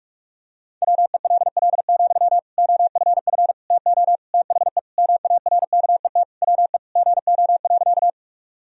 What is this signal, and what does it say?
Morse code, heard as a keyed tone, about 44 wpm, 700 Hz. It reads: WELL, OFF TO THE MARKET WE GO.
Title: Sentences from Top 1000 Words - Review Part 3 - 45wpm